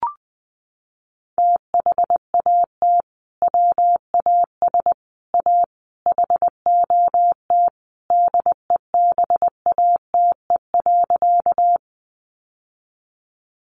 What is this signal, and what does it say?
Morse code, heard as a keyed tone, about 20 wpm, 700 Hz. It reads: THAT WAS A HOT DEBATE.
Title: Sentences from Top 1100 Words 20wpm